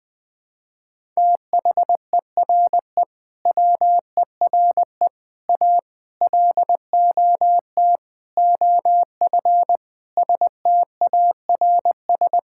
Morse code, heard as a keyed tone, about 20 wpm, 700 Hz. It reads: THERE WERE A LOT OF STARS
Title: Sentences with 5-Word Phrases - Spoken Beginning - 20wpm